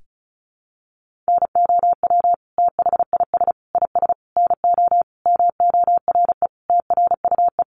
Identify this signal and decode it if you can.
Morse code, heard as a keyed tone, about 34 wpm, 700 Hz. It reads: NOW THIS IS NO MORE TRUE
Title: Sentences from Top 1400 Words 35wpm